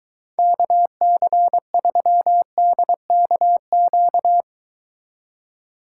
Morse code, heard as a keyed tone, about 23 wpm, 700 Hz. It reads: KC3DKQ